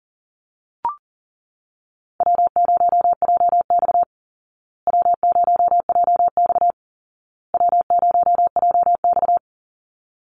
Morse code, heard as a keyed tone, about 40 wpm, 700 Hz. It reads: W0JX W0JX W0JX
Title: Easy Call Signs Repeated 3x - 40wpm